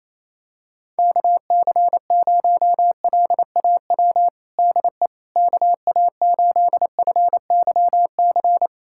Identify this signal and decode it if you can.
Morse code, heard as a keyed tone, about 28 wpm, 700 Hz. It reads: KC0LAW DE KA8FYC
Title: USA Callsign DE Callsign - 6 Characters - 28wpm